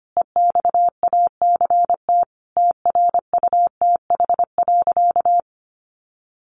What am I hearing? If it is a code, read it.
Morse code, heard as a keyed tone, about 25 wpm, 700 Hz. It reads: EXACT TRUTH.
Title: Sentences from Top 1300 Words 25wpm